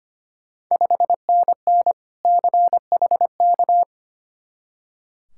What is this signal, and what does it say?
Morse code, heard as a keyed tone, about 25 wpm, 700 Hz. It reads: 5NN CHK
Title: Ontario QSO Party Exchange 25wpm